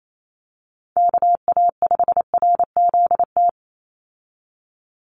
Morse code, heard as a keyed tone, about 28 wpm, 700 Hz. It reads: KA5RZT
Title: USA Call Signs 28wpm